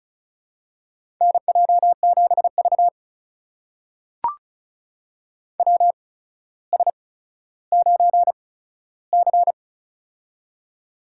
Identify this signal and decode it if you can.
Morse code, heard as a keyed tone, about 36 wpm, spaced as for 15 wpm, 700 Hz. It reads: NJ7V WS9C
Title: USA Call Signs - 4 Characters - 8x Character Spacing - 35wpm